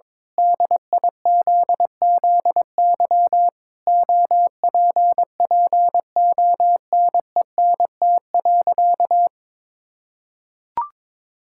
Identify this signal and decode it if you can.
Morse code, heard as a keyed tone, about 22 wpm, 700 Hz. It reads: DIZZY OPPONENT.
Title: Pangram Challenge v2 - 22wpm